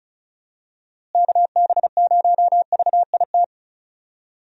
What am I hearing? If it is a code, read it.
Morse code, heard as a keyed tone, about 36 wpm, 700 Hz. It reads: KB0VIT
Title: USA Call Signs - 6 Characters - Repeated 3x - 35wpm